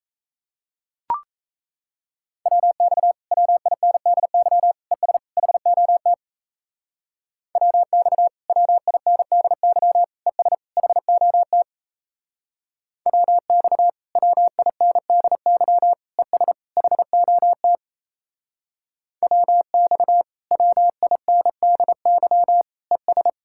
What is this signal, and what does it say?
Morse code, heard as a keyed tone, about 34 wpm, 700 Hz. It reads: WX WINDY ES HOT WX WINDY ES HOT WX WINDY ES HOT WX WINDY ES
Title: Speed-Racing Common QSO Phrases Encore - 28wpm